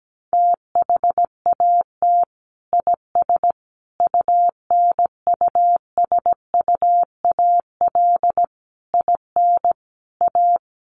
Morse code, heard as a keyed tone, about 17 wpm, 700 Hz. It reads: THAT IS UNUSUAL IN A
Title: Sentences from Top 1400 Words 17wpm